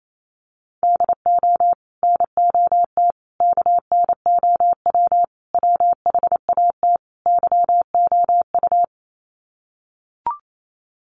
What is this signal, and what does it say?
Morse code, heard as a keyed tone, about 28 wpm, 700 Hz. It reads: DO NOT KNOW WHAT YOU
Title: Sets of 5 Words 28wpm